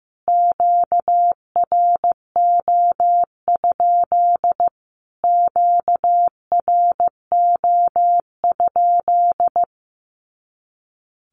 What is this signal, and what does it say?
Morse code, heard as a keyed tone, about 15 wpm, 700 Hz. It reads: QRO? QRO?
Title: Q-Codes Repeated 3x 15wpm